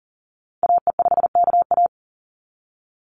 Morse code, heard as a keyed tone, about 40 wpm, 700 Hz. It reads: AE5KA
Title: NA SOTA Activator Callsigns 40wpm